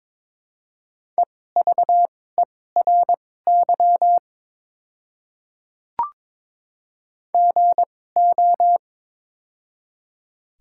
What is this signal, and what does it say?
Morse code, heard as a keyed tone, about 22 wpm, spaced as for 16 wpm, 700 Hz. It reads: EVERY GO
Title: 2x Character Spacing - Top 300 Words - 22wpm